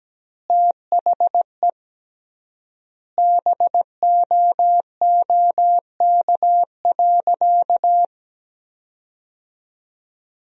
Morse code, heard as a keyed tone, about 17 wpm, 700 Hz. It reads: THE BOOK.
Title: 3x Word-Spacing - Sentences from Top 300 Words - 17wpm